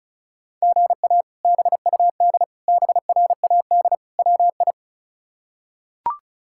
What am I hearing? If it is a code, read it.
Morse code, heard as a keyed tone, about 34 wpm, 700 Hz. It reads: GA BUD BRAD WI
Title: K1USN SST QSO Practice - 35wpm